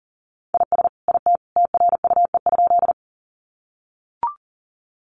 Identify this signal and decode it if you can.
Morse code, heard as a keyed tone, about 40 wpm, 700 Hz. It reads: IS IT TRUE?
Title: Sentences from Top 400 Words 40wpm